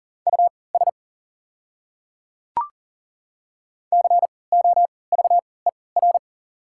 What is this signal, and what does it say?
Morse code, heard as a keyed tone, about 40 wpm, spaced as for 23 wpm, 700 Hz. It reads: US COVER
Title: 3x Character Spacing - Top 300 Words - 40wpm